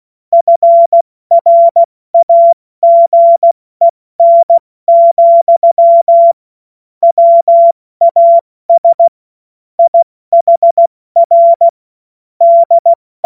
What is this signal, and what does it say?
Morse code, heard as a keyed tone, about 16 wpm, 680 Hz. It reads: FRAGEN, WAS IHR D
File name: Morsekurs / Witze / Witz 22. August 2023 70 F80